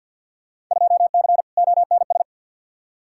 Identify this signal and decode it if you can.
Morse code, heard as a keyed tone, about 47 wpm, 700 Hz. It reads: 2C ONS
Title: Field Day Exchanges Encore - 50wpm